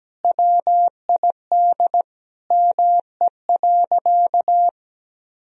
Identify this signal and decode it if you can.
Morse code, heard as a keyed tone, about 17 wpm, 700 Hz. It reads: WID ME.
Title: Sentences from Top 500 Words and Abbreviations 17wpm